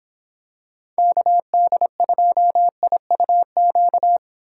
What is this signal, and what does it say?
Morse code, heard as a keyed tone, about 25 wpm, 700 Hz. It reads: KD2IUQ